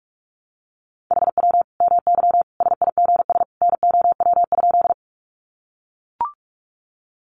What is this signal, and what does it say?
Morse code, heard as a keyed tone, about 44 wpm, 700 Hz. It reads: HW MY SIGS NOW?
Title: Common QSO Phrases 45wpm